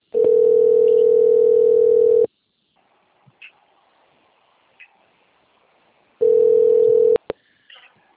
Ringback tone (North America), 440 + 480 Hz.
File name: rg-1600-unknown- - - .49340